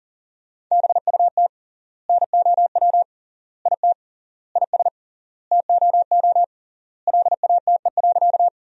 Morse code, heard as a keyed tone, about 40 wpm, 700 Hz. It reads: BUT NOW IT IS TOO LATE.
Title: 3x Word-Spacing - Sentences from Top 300 Words - 40wpm